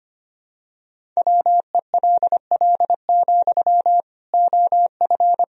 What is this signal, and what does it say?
Morse code, heard as a keyed tone, about 25 wpm, 700 Hz. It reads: WELL, OF
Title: Sentences from Top 1000 Words - Review Part 1 - 25wpm